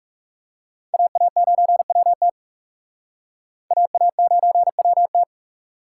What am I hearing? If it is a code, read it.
Morse code, heard as a keyed tone, about 40 wpm, 700 Hz. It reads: AA9WT AA9WT